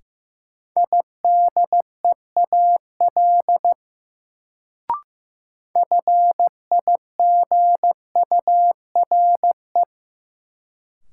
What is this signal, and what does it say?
Morse code, heard as a keyed tone, about 16 wpm, 700 Hz. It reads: IDEAL FIGURE
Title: Sentences from Top 1400 Words - New Word Drill Alt - 15wpm